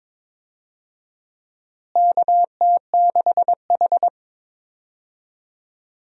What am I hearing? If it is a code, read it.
Morse code, heard as a keyed tone, about 22 wpm, 700 Hz. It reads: KT6H
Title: USA Call Signs - 4 Characters - Repeated 2x - 22wpm